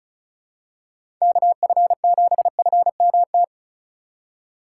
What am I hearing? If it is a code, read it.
Morse code, heard as a keyed tone, about 36 wpm, 700 Hz. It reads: KF7FMT
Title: USA Call Signs - 6 Characters - 3x Character Spacing - 35wpm